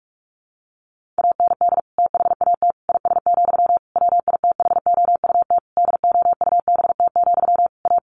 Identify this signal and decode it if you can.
Morse code, heard as a keyed tone, about 44 wpm, 700 Hz. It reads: AND THAT IS, WITHOUT DOUBT, A
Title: Sentences from Top 1500 Words 45wpm